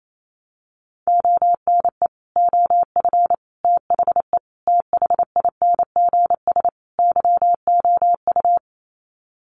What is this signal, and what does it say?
Morse code, heard as a keyed tone, about 28 wpm, 700 Hz. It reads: ONE OF THE THINGS YOU
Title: Sets of 5 Words Encore2 28wpm